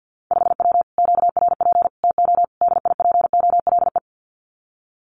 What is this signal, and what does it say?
Morse code, heard as a keyed tone, about 50 wpm, 700 Hz. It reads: 5W QRP TO DIPOLE
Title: Common QSO Phrases Encore 50wpm